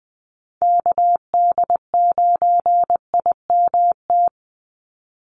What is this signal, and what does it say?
Morse code, heard as a keyed tone, about 20 wpm, 700 Hz. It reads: KD9IMT
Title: USA Call Signs 20wpm